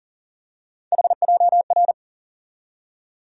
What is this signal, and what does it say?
Morse code, heard as a keyed tone, about 40 wpm, 700 Hz. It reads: HJR
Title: USA Call Sign Postfixes - 3 Letters 40wpm